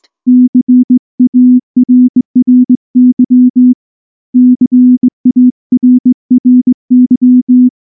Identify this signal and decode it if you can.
Morse code, heard as a keyed tone, about 20 wpm, 250 Hz. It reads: CARRY CARRY
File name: morse-training-20WPM